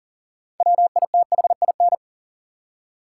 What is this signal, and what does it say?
Morse code, heard as a keyed tone, about 40 wpm, 700 Hz. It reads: WITHIN
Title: Six Letter Words - 40wpm